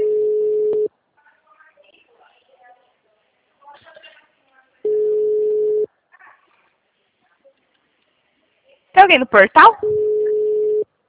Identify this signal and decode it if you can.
Ringback tone (Europe, 425 Hz), with speech.
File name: out- -2006- - - .38